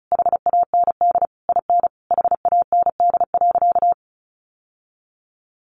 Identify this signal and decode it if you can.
Morse code, heard as a keyed tone, about 34 wpm, 700 Hz. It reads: HAND IN HAND.
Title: Sentences from Top 1100 Words 35wpm